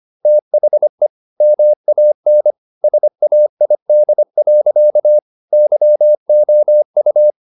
Morse code, heard as a keyed tone, about 25 wpm, 600 Hz. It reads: THE MAN SAID. YOU
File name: the old man and the sea part 1 25wpm